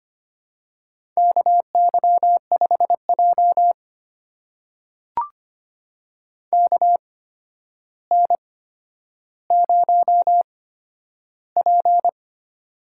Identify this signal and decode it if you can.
Morse code, heard as a keyed tone, about 25 wpm, spaced as for 11 wpm, 700 Hz. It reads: KY5J KN0P
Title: USA Call Signs - 4 Characters - 8x Character Spacing - 25wpm